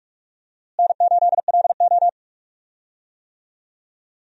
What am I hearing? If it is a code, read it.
Morse code, heard as a keyed tone, about 46 wpm, 700 Hz. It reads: N8LO
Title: USA Call Signs - 4 Characters - Repeated 2x - 45wpm